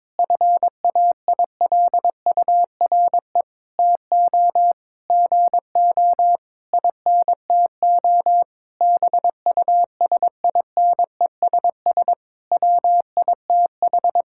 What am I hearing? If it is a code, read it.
Morse code, heard as a keyed tone, about 21 wpm, 700 Hz. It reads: FAILURE TO GO INTO BUSINESS WITH
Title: Sentences from Top 1000 Words - Review Part 6 - 22wpm